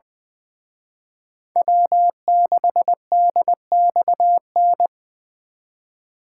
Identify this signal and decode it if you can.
Morse code, heard as a keyed tone, about 23 wpm, 700 Hz. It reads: W6DXN
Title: Speed Racing - USA 1x3 Call Signs 20wpm